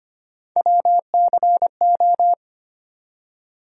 Morse code, heard as a keyed tone, about 25 wpm, 700 Hz. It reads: WCO
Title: USA Call Sign Postfixes - 3 Letters 25wpm